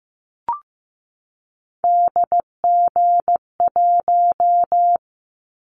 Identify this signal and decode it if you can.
Morse code, heard as a keyed tone, about 15 wpm, 700 Hz. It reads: DG1
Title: Callsign Prefixes for Top 25 DXCC - 15wpm